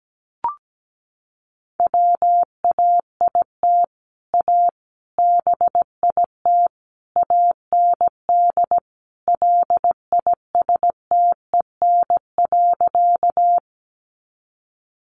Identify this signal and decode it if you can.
Morse code, heard as a keyed tone, about 17 wpm, 700 Hz. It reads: WAIT A BIT AND LISTEN.
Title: Sentences from Top 900 Words 17wpm